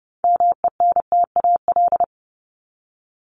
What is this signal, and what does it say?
Morse code, heard as a keyed tone, about 30 wpm, 700 Hz. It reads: MENTAL
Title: Top 501-750 Words 30wpm